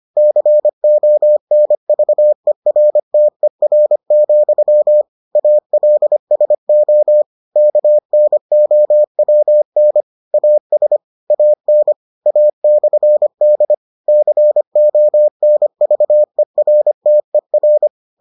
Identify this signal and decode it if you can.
Morse code, heard as a keyed tone, about 25 wpm, 600 Hz. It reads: CONVERTER, ALSO KNOWN AS AN A/D CONVERTER